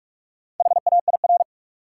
Morse code, heard as a keyed tone, about 43 wpm, 700 Hz. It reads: HAIR
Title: Four Letter Words - 45wpm